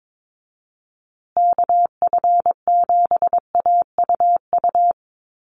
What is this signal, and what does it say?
Morse code, heard as a keyed tone, about 22 wpm, 700 Hz. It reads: KF7AUU
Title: USA Call Signs Repeated 3x 22wpm